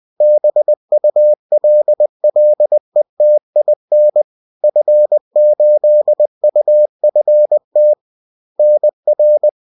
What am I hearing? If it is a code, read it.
Morse code, heard as a keyed tone, about 20 wpm, 600 Hz. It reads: BULLETIN F8UFT NR